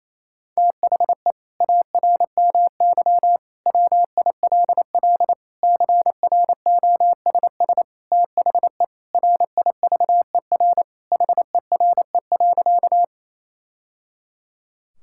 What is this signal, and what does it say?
Morse code, heard as a keyed tone, about 28 wpm, 700 Hz. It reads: THE ARMY WILL CROSS THE RIVER HERE.
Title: Sentences from Top 1000 Words - Review Part 5 - 28wpm